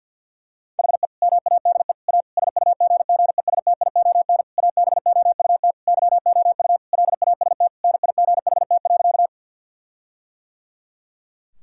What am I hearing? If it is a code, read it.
Morse code, heard as a keyed tone, about 50 wpm, 700 Hz. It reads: HE MADE A SUGGESTION ABOUT YOU LAST NIGHT.
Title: Sentences from Top 1600 Words 50wpm